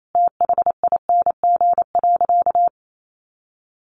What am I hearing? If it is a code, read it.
Morse code, heard as a keyed tone, about 28 wpm, 700 Hz. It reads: THING.
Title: Sentences from Top 1500 Words 28wpm